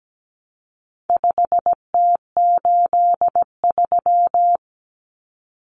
Morse code, heard as a keyed tone, about 17 wpm, 700 Hz. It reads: 5T83
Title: Common Cut Numbers - 4-Digit 17wpm